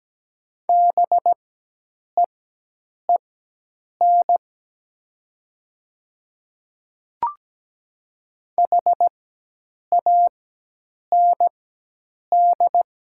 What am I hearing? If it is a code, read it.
Morse code, heard as a keyed tone, about 17 wpm, spaced as for 8 wpm, 700 Hz. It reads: BEEN HAND
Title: 4x Character Spacing - Top 200 Words - 17wpm